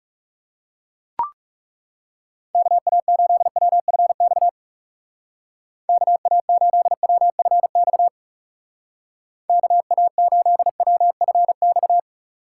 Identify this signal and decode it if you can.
Morse code, heard as a keyed tone, about 37 wpm, 700 Hz. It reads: KA8WFX KA8WFX KA8WFX
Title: Speed Racing - USA 2x3 Call Signs 30wpm